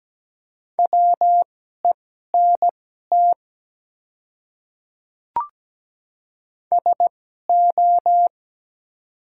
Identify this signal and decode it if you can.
Morse code, heard as a keyed tone, about 17 wpm, spaced as for 12 wpm, 700 Hz. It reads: WENT SO
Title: 2x Character Spacing - Top 300 Words - 17wpm